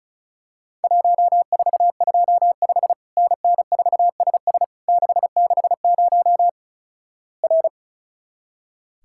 Morse code, heard as a keyed tone, about 36 wpm, 700 Hz. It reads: 1425 NN4SS 660 R
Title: Worked All Europe DX Contest - QTC Exchange - 35wpm